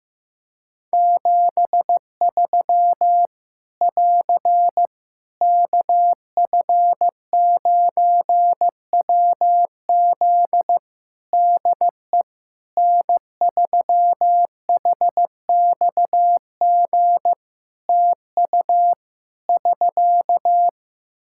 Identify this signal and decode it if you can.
Morse code, heard as a keyed tone, about 15 wpm, 700 Hz. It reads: 73 <AR> KF9WZ DE N3HXG TU <SK>
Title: Practice QSO - Exchange 4 - Ending - 15wpm